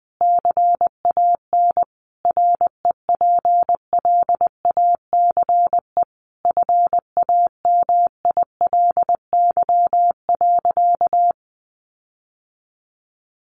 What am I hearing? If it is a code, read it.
Morse code, heard as a keyed tone, about 20 wpm, 700 Hz. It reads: CAN REPLACE FAMILY.
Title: Sentences from Top 1400 Words 20wpm